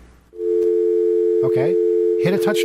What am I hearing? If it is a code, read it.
Dial tone (North America), 350 + 440 Hz, with speech.
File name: mp3files / off the hook overtime -128